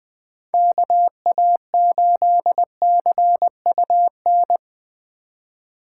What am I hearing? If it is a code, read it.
Morse code, heard as a keyed tone, about 19 wpm, 700 Hz. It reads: KA8CUN
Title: Speed Racing - USA 2x3 Call Signs 17wpm